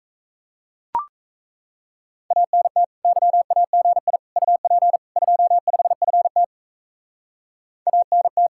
Morse code, heard as a keyed tone, about 34 wpm, 700 Hz. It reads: ANT YAGI UP 25FT ANT
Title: Speed-Racing Common QSO Phrases - 28wpm